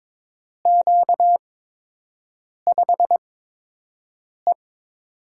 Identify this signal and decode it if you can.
Morse code, heard as a keyed tone, about 22 wpm, spaced as for 6 wpm, 700 Hz. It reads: Q5E
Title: USA Call Signs - 4 Characters - 8x Character Spacing - 22wpm